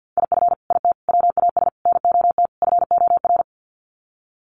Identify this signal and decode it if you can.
Morse code, heard as a keyed tone, about 50 wpm, 700 Hz. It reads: IF IT WAS NOT FOR
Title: Sets of 5 Words Encore 50wpm